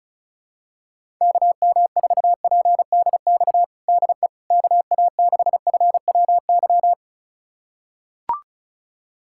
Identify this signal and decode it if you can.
Morse code, heard as a keyed tone, about 34 wpm, 700 Hz. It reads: KM4PDX DE KA6FWY
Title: USA Callsign DE Callsign - 6 Characters - 35wpm